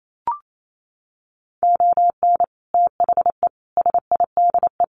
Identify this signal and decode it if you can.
Morse code, heard as a keyed tone, about 28 wpm, 700 Hz. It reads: ON THE SIDE
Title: Sets of 4 Words 28wpm